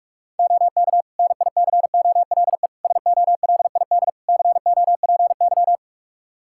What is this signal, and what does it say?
Morse code, heard as a keyed tone, about 42 wpm, 700 Hz. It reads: OK NICOLE SOLID COPY
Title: Practice QSO - Exchange 3 - More Info - 45wpm